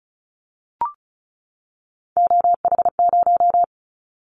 Morse code, heard as a keyed tone, about 36 wpm, 700 Hz. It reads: OH0
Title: Callsign Prefixes for Top 100 DXCC - 35wpm